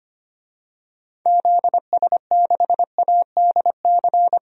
Morse code, heard as a keyed tone, about 25 wpm, 700 Hz. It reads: ZS6ADC
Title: DXCC-100 Call Signs - 6 Characters - 25wpm